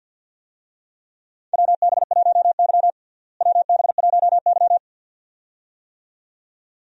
Morse code, heard as a keyed tone, about 50 wpm, 700 Hz. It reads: WB1Y WB1Y
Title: USA Call Signs - 4 Characters - Repeated 2x - 50wpm